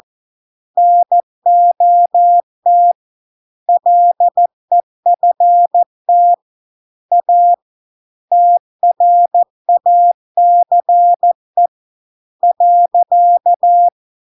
Morse code, heard as a keyed tone, about 14 wpm, 700 Hz. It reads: NOT LEFT A TRACE .